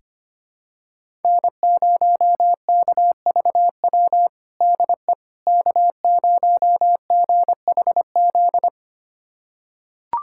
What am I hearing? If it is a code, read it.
Morse code, heard as a keyed tone, about 25 wpm, 700 Hz. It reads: N0KVW DE K0GHZ
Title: USA Callsign DE Callsign - 5 Characters - 25wpm